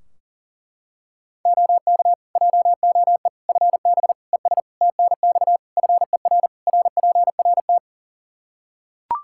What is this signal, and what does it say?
Morse code, heard as a keyed tone, about 40 wpm, 700 Hz. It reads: OK JOE FB ES TNX FER RPRT
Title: K1USN SST QSO Practice 40wpm